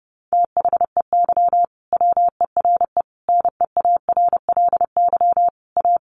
Morse code, heard as a keyed tone, about 30 wpm, 700 Hz. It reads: THEY WERE NEARLY A
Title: Sentences from Top 1300 Words 30wpm